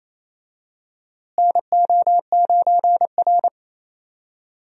Morse code, heard as a keyed tone, about 27 wpm, 700 Hz. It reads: NO9R
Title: USA Call Signs - 4 Characters - Repeated 2x - 28wpm